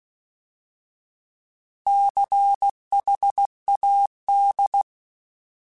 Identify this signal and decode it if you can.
Morse code, heard as a keyed tone, about 16 wpm, 800 Hz. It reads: CHAD
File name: Alphabetical / Alphabetical mp3s / Cs @15